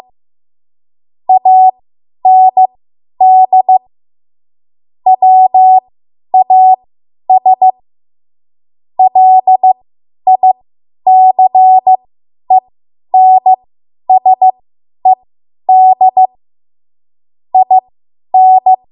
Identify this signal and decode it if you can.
Morse code, heard as a keyed tone, about 15 wpm, spaced as for 10 wpm, 750 Hz. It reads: AND WAS LICENSED IN